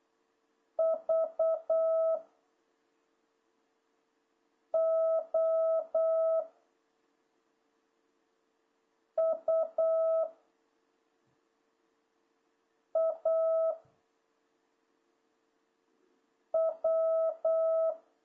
Morse code, heard as a keyed tone, about 8 wpm, 660 Hz. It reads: V O U A W
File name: jh8cbh / 2026cw